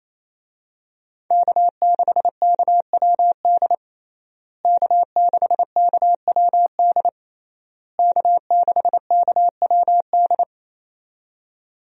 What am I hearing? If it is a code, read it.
Morse code, heard as a keyed tone, about 27 wpm, 700 Hz. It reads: K6KWD K6KWD K6KWD
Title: USA Call Signs - 5 Characters - Repeated 3x - 28wpm